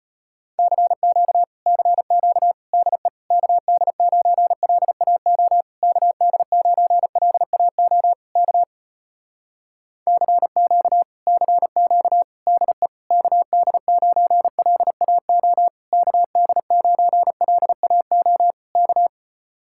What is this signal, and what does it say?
Morse code, heard as a keyed tone, about 36 wpm, 700 Hz. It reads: CQ CQ DE KD9LAO KD9LAO K CQ CQ DE KD9LAO KD9LAO K